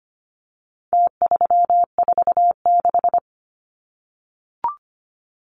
Morse code, heard as a keyed tone, about 25 wpm, 700 Hz. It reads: T346
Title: Numbers - Contest Style - 4-Digit 25wpm